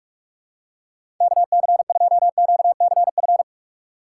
Morse code, heard as a keyed tone, about 47 wpm, 700 Hz. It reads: KC2QCF
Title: USA Call Signs - 6 Characters - 4x Character Spacing - 45wpm